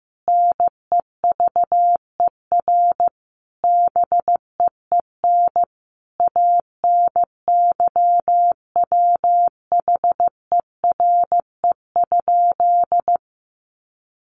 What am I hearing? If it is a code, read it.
Morse code, heard as a keyed tone, about 14 wpm, 700 Hz. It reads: NEVER BEEN ANYWHERE?
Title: Sentences from Top 1400 Words 15wpm